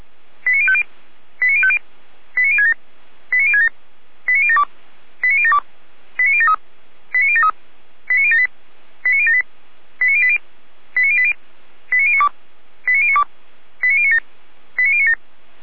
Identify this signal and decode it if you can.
Five-tone selective call (ZVEI), heard as quick runs of five tones, tones of about 70 ms. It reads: 89050 89050 89067 89067 89072 89072 89073 89073 89078 89078 89080 89080 89092 89092 89097 89097
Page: SDR/ZVEI